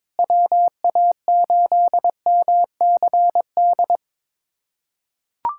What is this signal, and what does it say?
Morse code, heard as a keyed tone, about 22 wpm, 700 Hz. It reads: WA8MCD
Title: USA Callsign DE Callsign 22wpm